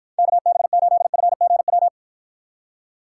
Morse code, heard as a keyed tone, about 52 wpm, 700 Hz. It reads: KB8FGW